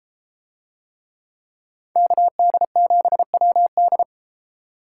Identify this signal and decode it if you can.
Morse code, heard as a keyed tone, about 31 wpm, 700 Hz. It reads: KD7WD